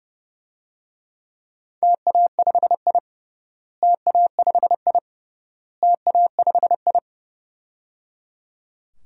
Morse code, heard as a keyed tone, about 32 wpm, 700 Hz. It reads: TA5I TA5I TA5I
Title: DXCC-100 Call Signs - 4 Characters - Repeated 3x - 30wpm